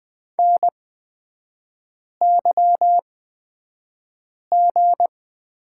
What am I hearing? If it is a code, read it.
Morse code, heard as a keyed tone, about 20 wpm, spaced as for 5 wpm, 690 Hz. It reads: NYG
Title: 229 Lesson 37e - Letter x - Three Characters 20wpm